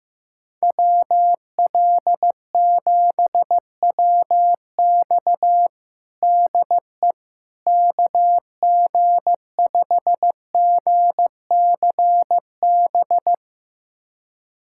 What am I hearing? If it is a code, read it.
Morse code, heard as a keyed tone, about 15 wpm, 700 Hz. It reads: WL7WX DE KG5GCB
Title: USA Callsign DE Callsign 15wpm